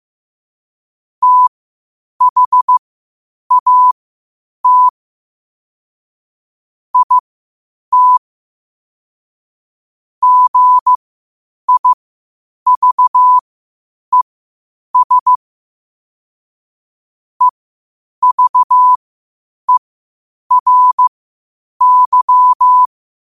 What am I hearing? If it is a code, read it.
Morse code, heard as a keyed tone, about 15 wpm, spaced as for 8 wpm, 1000 Hz. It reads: THAT IT GIVES EVERY
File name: qotdfiles / 8wpm